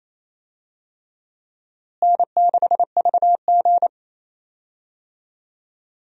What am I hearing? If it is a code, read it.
Morse code, heard as a keyed tone, about 28 wpm, 700 Hz. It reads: N6VG
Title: USA Call Signs - 4 Characters - Repeated 3x - 28wpm